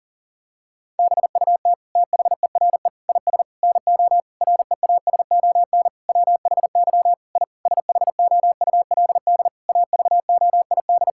Morse code, heard as a keyed tone, about 40 wpm, 700 Hz. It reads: BUT THERE IS NO REASON WHY I SHOULD AVOID
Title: Sentences from Top 1000 Words - Review Part 6 - 40wpm